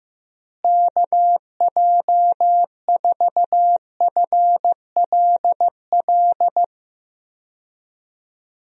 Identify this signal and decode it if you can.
Morse code, heard as a keyed tone, about 15 wpm, 700 Hz. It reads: KJ4FLL
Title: USA Call Signs - 6 Characters - Repeated 3x - 15wpm